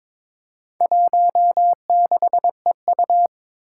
Morse code, heard as a keyed tone, about 23 wpm, 700 Hz. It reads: 16EU
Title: Four Challenging Letter-Number Combinations 22wpm